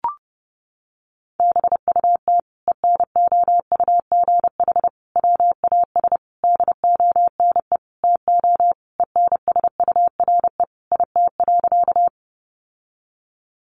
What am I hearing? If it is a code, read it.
Morse code, heard as a keyed tone, about 28 wpm, 700 Hz. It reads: BUT ENOUGH WAS DONE TO ENSURE IT.
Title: Sentences from Top 1100 Words 30wpm